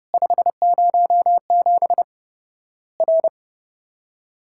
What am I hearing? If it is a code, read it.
Morse code, heard as a keyed tone, about 30 wpm, 700 Hz. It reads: 507 R
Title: Worked All Europe DX Contest - QTC Exchange - 30wpm